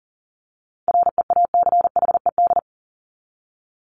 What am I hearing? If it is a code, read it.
Morse code, heard as a keyed tone, about 40 wpm, 700 Hz. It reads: REACHED
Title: ED Suffix - Top 100 Words with Base Words 40wpm